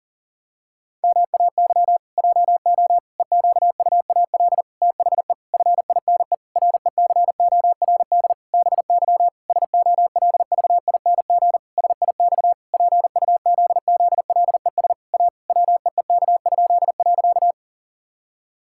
Morse code, heard as a keyed tone, about 37 wpm, 700 Hz. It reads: MAY JO EQUAL THE FINE RECORD BY SOLVING SIX PUZZLES A WEEK?.
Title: Pangram Challenge v2 - 40wpm